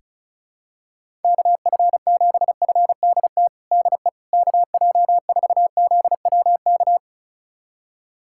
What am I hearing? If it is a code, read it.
Morse code, heard as a keyed tone, about 34 wpm, 700 Hz. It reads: KF7FDT DE KJ4ZWK
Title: USA Callsign DE Callsign - 6 Characters - 35wpm